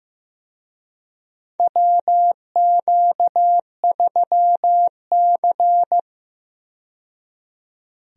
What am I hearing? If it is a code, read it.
Morse code, heard as a keyed tone, about 16 wpm, 700 Hz. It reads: WQ3C
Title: USA Call Signs - 4 Characters - 15wpm